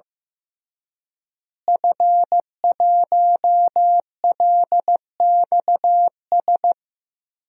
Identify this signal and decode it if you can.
Morse code, heard as a keyed tone, about 15 wpm, 700 Hz. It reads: F1LXS
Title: DXCC-100 Call Signs - 5 Characters - Repeated 3x - 15wpm